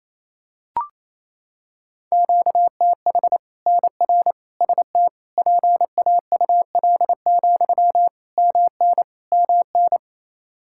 Morse code, heard as a keyed tone, about 28 wpm, 700 Hz. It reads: QTH NR ST PAUL, MN MN
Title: Practice QSO - Exchange 1 - Beginning - 28wpm